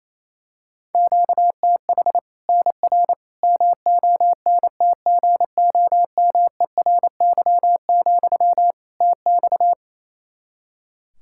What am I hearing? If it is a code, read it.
Morse code, heard as a keyed tone, about 28 wpm, 700 Hz. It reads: QTH NR MONTGOMERY, TX
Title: Practice QSO - Exchange 1 - Beginning - 28wpm